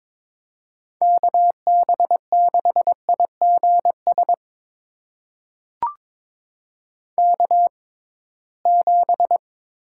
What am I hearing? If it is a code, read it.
Morse code, heard as a keyed tone, about 22 wpm, spaced as for 17 wpm, 700 Hz. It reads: KB6IGS K7
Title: USA Call Signs - 6x Character Spacing - 22wpm